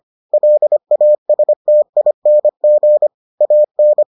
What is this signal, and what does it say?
Morse code, heard as a keyed tone, about 25 wpm, 600 Hz. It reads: LASTING AN